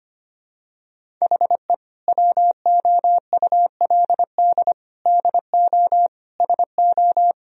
Morse code, heard as a keyed tone, about 25 wpm, 700 Hz. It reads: HE WOULD DO SO
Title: Sentences from Top 1000 Words - Review Part 1 - 25wpm